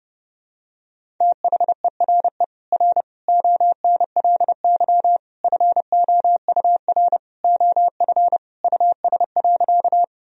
Morse code, heard as a keyed tone, about 30 wpm, 700 Hz. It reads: THERE R ONLY FOUR OF US.
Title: Sentences from Top 300 Words and Abbreviations 30wpm